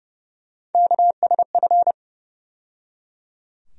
Morse code, heard as a keyed tone, about 30 wpm, 700 Hz. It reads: KSF
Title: USA Call Sign Postfixes - 3 Letters 30wpm